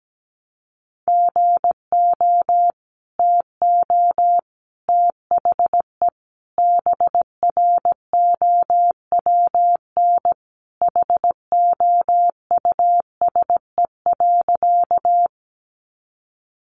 Morse code, heard as a keyed tone, about 17 wpm, 700 Hz. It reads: GO TO THE BROWN HOUSE.
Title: Sentences from Top 1200 Words 17wpm